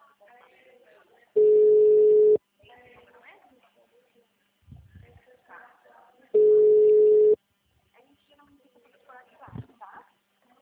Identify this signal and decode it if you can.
Ringback tone (Europe, 425 Hz), with speech.